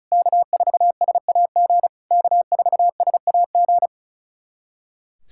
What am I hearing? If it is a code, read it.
Morse code, heard as a keyed tone, about 36 wpm, 700 Hz. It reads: K4SAG K4SAG
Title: SOTA QSO Practice 35wpm